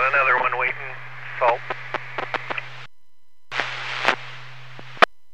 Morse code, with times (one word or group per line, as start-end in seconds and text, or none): none